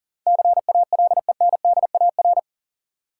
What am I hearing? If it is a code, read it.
Morse code, heard as a keyed tone, about 40 wpm, 700 Hz. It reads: CALENDAR
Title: Sentences from Top 1600 Words - New Word Drill Alt - 40wpm